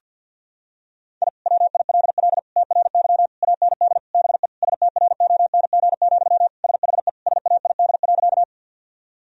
Morse code, heard as a keyed tone, about 46 wpm, 700 Hz. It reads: I WILL TRY AND BE STRONG, SHE SAID.
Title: Sentences from Top 1000 Words - Review Part 3 - 50wpm